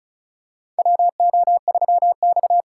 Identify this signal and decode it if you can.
Morse code, heard as a keyed tone, about 34 wpm, 700 Hz. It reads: WO3X
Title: USA Call Signs - 4 Characters - 8x Character Spacing - 35wpm